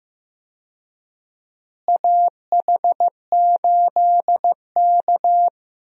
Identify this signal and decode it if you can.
Morse code, heard as a keyed tone, about 15 wpm, 700 Hz. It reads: AH8K
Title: USA Call Signs - 4 Characters - Repeated 3x - 15wpm